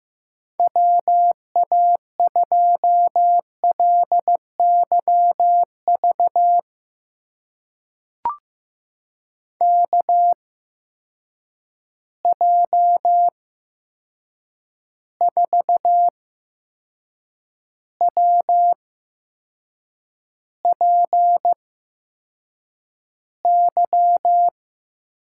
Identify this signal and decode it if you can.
Morse code, heard as a keyed tone, about 15 wpm, spaced as for 6 wpm, 700 Hz. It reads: WA2LYV KJ4WPY
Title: USA Call Signs - 8x Character Spacing - 15wpm